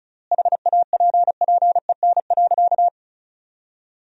Morse code, heard as a keyed tone, about 36 wpm, 700 Hz. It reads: HAPPEN.
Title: 4x Word-Spacing - Sentences from Top 400 Words - 35wpm